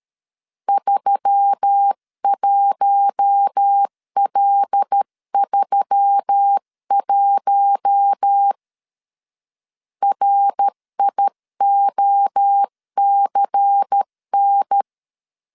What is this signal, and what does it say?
Morse code, heard as a keyed tone, about 13 wpm, 800 Hz. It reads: 31L31 RIOCN
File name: CW / DET / 1 6 1500C S C 12wpm 3533